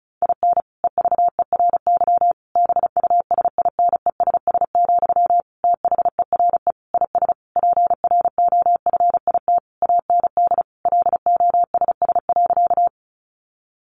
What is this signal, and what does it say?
Morse code, heard as a keyed tone, about 34 wpm, 700 Hz. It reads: IN EVERY BUSINESS, THERE IS PROFIT AND LOSS.
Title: Sentences from Top 1400 Words 35wpm